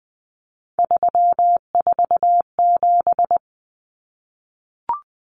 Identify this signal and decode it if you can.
Morse code, heard as a keyed tone, about 20 wpm, 700 Hz. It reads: 347
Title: Numbers - 3-Digit 20wpm